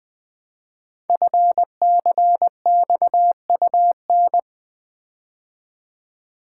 Morse code, heard as a keyed tone, about 20 wpm, 700 Hz. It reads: FCXUN
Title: Code Groups - 5 Letters - 20wpm